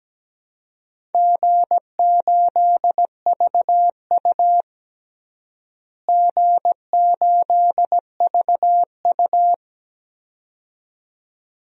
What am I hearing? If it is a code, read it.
Morse code, heard as a keyed tone, about 17 wpm, 700 Hz. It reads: G8VU G8VU
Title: DXCC-100 Call Signs - 4 Characters - Repeated 2x - 17wpm